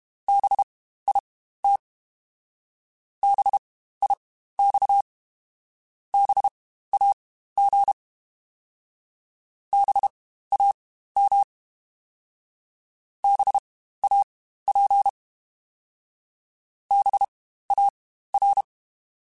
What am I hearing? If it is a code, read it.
Morse code, heard as a keyed tone, about 32 wpm, spaced as for 15 wpm, 800 Hz. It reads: BIT BIX BAG BAM BAP BAR
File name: Spaced Alphas / Spaced Alphas mp3s / Bs sp @30